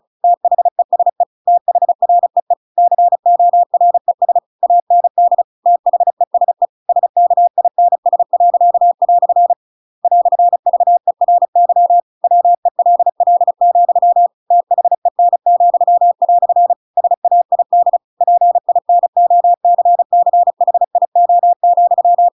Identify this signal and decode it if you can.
Morse code, heard as a keyed tone, about 34 wpm, 700 Hz. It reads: THESE THREE CORES AND THESE SKINS." "VERY WELL, THEN," SAID PINOCCHIO,
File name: Pinocchio / Pinocchio ch7 35 35 ik1hge